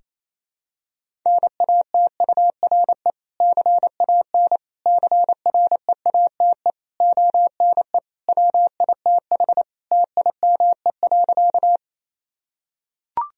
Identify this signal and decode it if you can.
Morse code, heard as a keyed tone, about 27 wpm, 700 Hz. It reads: NATURE CAN CREATE ONE WITH TIME.
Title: Sentences from Top 1000 Words - Review Part 3 - 28wpm